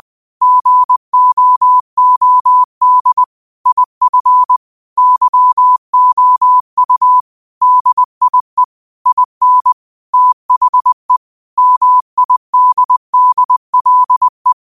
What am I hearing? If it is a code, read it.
Morse code, heard as a keyed tone, about 20 wpm, 1000 Hz. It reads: GOOD IF YOU DIE IN THE MIDDLE